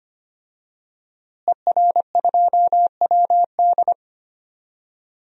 Morse code, heard as a keyed tone, about 25 wpm, 700 Hz. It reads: ER2WD
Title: DXCC-100 Call Signs - 5 Characters - Repeated 3x - 25wpm